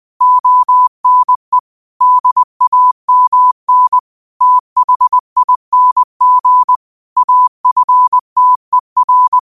Morse code, heard as a keyed tone, about 20 wpm, 1000 Hz. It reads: ONE DAMN THING AFTER
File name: qotdfiles / 20wpm